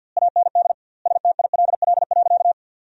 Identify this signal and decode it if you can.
Morse code, heard as a keyed tone, about 47 wpm, 700 Hz. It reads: AND STILL.
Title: 2x Word-Spacing - Sentences from Top 500 Words - 50wpm